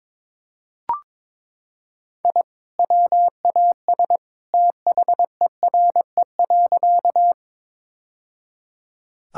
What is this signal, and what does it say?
Morse code, heard as a keyed tone, about 21 wpm, 700 Hz. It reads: I WAS THERE.
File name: practice / 22wpm / Sentences with 2-Word Phrases - 22wpm